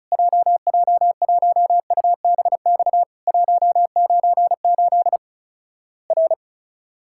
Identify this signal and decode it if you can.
Morse code, heard as a keyed tone, about 36 wpm, 700 Hz. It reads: JJ1UBX 198 R
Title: Worked All Europe DX Contest - QTC Exchange - 35wpm